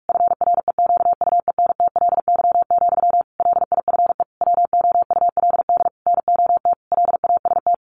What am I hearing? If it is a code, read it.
Morse code, heard as a keyed tone, about 44 wpm, 700 Hz. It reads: FREQUENTLY, LIFE WOULD NOT LAST
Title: Sentences from Top 1300 Words 45wpm